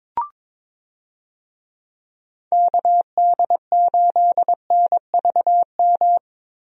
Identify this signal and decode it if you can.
Morse code, heard as a keyed tone, about 22 wpm, 700 Hz. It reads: KD8NVM
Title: USA Call Signs - 6 Characters - Repeated 2x - 22wpm